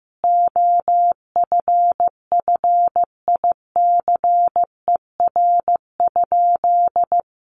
Morse code, heard as a keyed tone, about 15 wpm, 700 Hz. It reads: OFFICER?
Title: Sentences from Top 1200 Words 15wpm